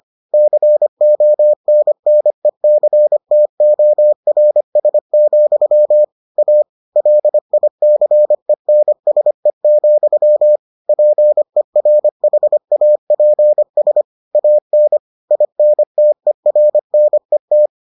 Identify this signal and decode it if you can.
Morse code, heard as a keyed tone, about 25 wpm, 600 Hz. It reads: CONNECTORS, A LICENSE, PERHAPS AN INTERNET